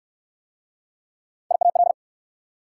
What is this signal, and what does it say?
Morse code, heard as a keyed tone, about 69 wpm, 690 Hz. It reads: IAL